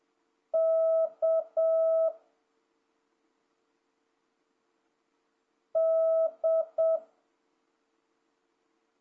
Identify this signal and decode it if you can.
Morse code, heard as a keyed tone, about 7 wpm, 650 Hz. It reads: K D
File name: jh8cbh / 2026cw